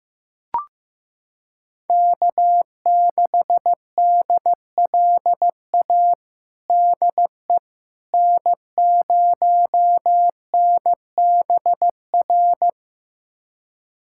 Morse code, heard as a keyed tone, about 15 wpm, 700 Hz. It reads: K6DLA DE N0NBR
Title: USA Callsign DE Callsign - 5 Characters - 15wpm